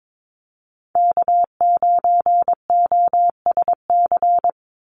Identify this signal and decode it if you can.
Morse code, heard as a keyed tone, about 22 wpm, 700 Hz. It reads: K9OSC
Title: NA SOTA Chaser Callsigns 22wpm